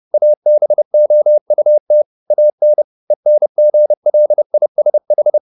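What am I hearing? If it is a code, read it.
Morse code, heard as a keyed tone, about 30 wpm, 600 Hz. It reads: ABOUT AN ENGLISH
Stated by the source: Morse code at 30 wpm, the novel Never by Ken Follett in English, from CW med Gnister.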